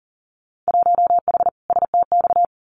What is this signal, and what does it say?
Morse code, heard as a keyed tone, about 41 wpm, 700 Hz. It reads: 1H STX
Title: Winter Field Day Exchanges Encore 40wpm